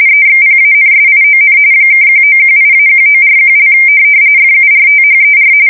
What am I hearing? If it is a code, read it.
RTTY, about 45.45 baud. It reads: EXPANSION (1918-1941) FURTHER
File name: fsk170 / usnavy4